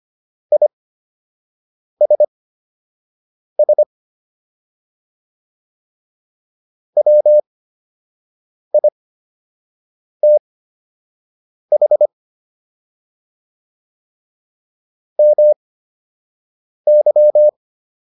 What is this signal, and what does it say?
Morse code, heard as a keyed tone, about 25 wpm, spaced as for 6 wpm, 600 Hz. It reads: ISS WITH MY